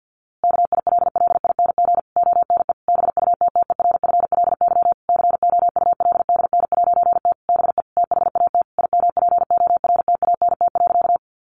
Morse code, heard as a keyed tone, about 50 wpm, 700 Hz. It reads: KILLING ONE BUTTERFLY COULDN'T BE THAT IMPORTANT.
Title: A Sound Of Thunder 50wpm